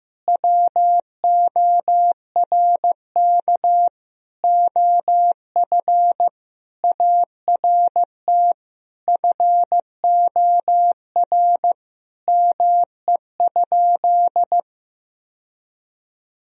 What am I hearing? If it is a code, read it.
Morse code, heard as a keyed tone, about 15 wpm, 700 Hz. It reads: WORK OF ART FOR ME?
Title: Sentences from Top 1000 Words - Review Part 3 - 15wpm